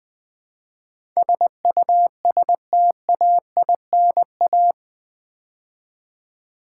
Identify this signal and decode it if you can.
Morse code, heard as a keyed tone, about 20 wpm, 700 Hz. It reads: SUSTAINA
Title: Build A Word - 4 Letters At A Time - 11-Letter Words - 20wpm